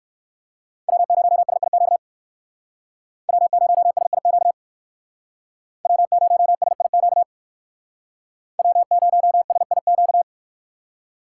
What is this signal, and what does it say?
Morse code, heard as a keyed tone, about 59 wpm, 700 Hz. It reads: W0SIQ W0SIQ W0SIQ W0SIQ